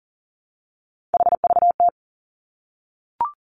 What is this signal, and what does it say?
Morse code, heard as a keyed tone, about 42 wpm, 700 Hz. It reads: HVT
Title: Three Challenging Letter-Number Combinations 40wpm